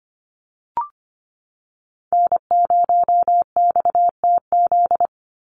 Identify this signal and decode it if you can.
Morse code, heard as a keyed tone, about 25 wpm, 690 Hz. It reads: N0XTZ
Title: USA Call Signs 25wpm